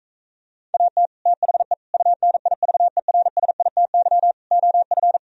Morse code, heard as a keyed tone, about 45 wpm, 700 Hz. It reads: AT THE UNIVERSITY OF